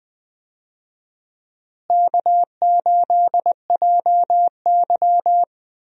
Morse code, heard as a keyed tone, about 20 wpm, 700 Hz. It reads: K8JY
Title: USA Call Signs - 4 Characters - Repeated 2x - 20wpm